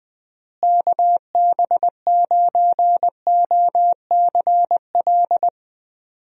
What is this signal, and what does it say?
Morse code, heard as a keyed tone, about 21 wpm, 700 Hz. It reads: KB9OCL
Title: USA Call Signs - 6 Characters - Repeated 3x - 20wpm